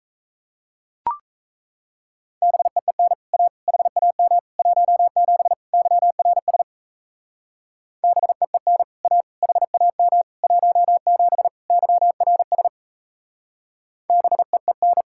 Morse code, heard as a keyed tone, about 34 wpm, 700 Hz. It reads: BEEN A HAM 17 YRS BEEN A HAM 17 YRS BEEN
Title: Speed-Racing Common QSO Phrases - 28wpm